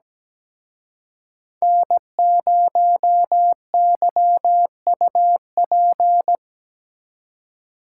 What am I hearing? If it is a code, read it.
Morse code, heard as a keyed tone, about 20 wpm, 700 Hz. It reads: N0YUP
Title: Speed Racing - USA 1x3 Call Signs 17wpm